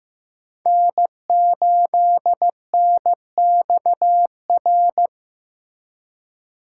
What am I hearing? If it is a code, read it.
Morse code, heard as a keyed tone, about 15 wpm, 700 Hz. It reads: N8NXR
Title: USA Call Signs - 5 Characters - 4x Character Spacing - 15wpm